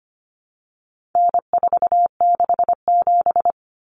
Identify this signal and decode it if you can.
Morse code, heard as a keyed tone, about 25 wpm, 700 Hz. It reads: N467
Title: Common Cut Numbers - 4-Digit 25wpm